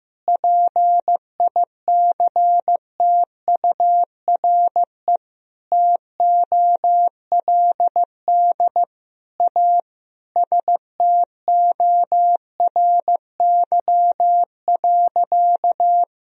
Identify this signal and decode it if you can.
Morse code, heard as a keyed tone, about 15 wpm, 700 Hz. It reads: PICTURE TOLD A STORY.
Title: Sentences from Top 1000 Words - Review Part 1 - 15wpm